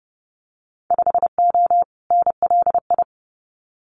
Morse code, heard as a keyed tone, about 30 wpm, 700 Hz. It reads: 5O NLI
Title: Winter Field Day Exchanges 30wpm